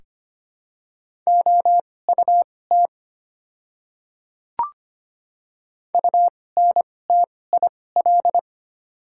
Morse code, heard as a keyed tone, about 25 wpm, spaced as for 18 wpm, 700 Hz. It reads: OUT UNTIL
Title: 2x Character Spacing - Top 300 Words - 25wpm